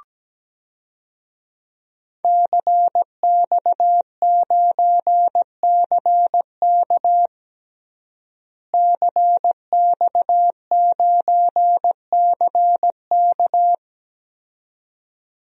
Morse code, heard as a keyed tone, about 17 wpm, 700 Hz. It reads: CX9CK CX9CK
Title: DXCC-100 Call Signs - 5 Characters - Repeated 2x - 17wpm